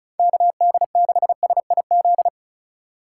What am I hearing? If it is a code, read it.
Morse code, heard as a keyed tone, about 34 wpm, 700 Hz. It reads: KD6SIZ
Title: USA Call Signs - 6 Characters - 4x Character Spacing - 35wpm